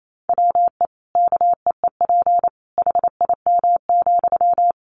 Morse code, heard as a keyed tone, about 28 wpm, 700 Hz. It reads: WE KEEP HIM,
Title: Sentences from Top 600 Words 28wpm